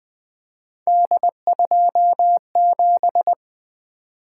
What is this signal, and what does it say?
Morse code, heard as a keyed tone, about 20 wpm, 700 Hz. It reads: D27
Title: Three Letter-Number Combinations - 20wpm